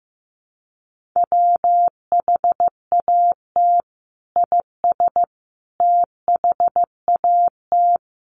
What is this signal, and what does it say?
Morse code, heard as a keyed tone, about 15 wpm, 700 Hz. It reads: WHAT IS THAT
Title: Sentences from Top 200 Words 15wpm